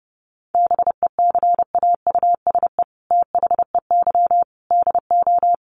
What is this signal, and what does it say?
Morse code, heard as a keyed tone, about 30 wpm, 700 Hz. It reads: BECAUSE THEY DO
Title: Sets of 5 Words Encore2 30wpm